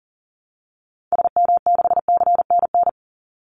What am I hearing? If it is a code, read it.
Morse code, heard as a keyed tone, about 40 wpm, 700 Hz. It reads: SM6CNN
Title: Call Signs - 40wpm